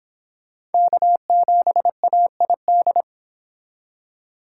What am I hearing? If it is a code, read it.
Morse code, heard as a keyed tone, about 30 wpm, 700 Hz. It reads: K7AID